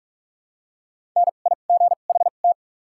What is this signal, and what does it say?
Morse code, heard as a keyed tone, about 47 wpm, spaced as for 33 wpm, 700 Hz. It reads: NIGHT
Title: 2x Character Spacing - Top 500 Words - 45wpm